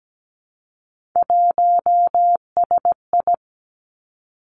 Morse code, heard as a keyed tone, about 17 wpm, 700 Hz. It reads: 1SI
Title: Three Challenging Letter-Number Combinations 17wpm